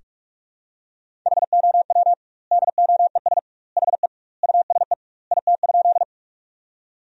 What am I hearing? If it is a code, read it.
Morse code, heard as a keyed tone, about 46 wpm, 700 Hz. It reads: HOW DOES HE USE IT?
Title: 2x Word-Spacing - Sentences from Top 200 Words - 45wpm